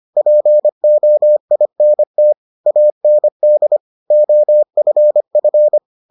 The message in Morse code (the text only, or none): POINT AND OFF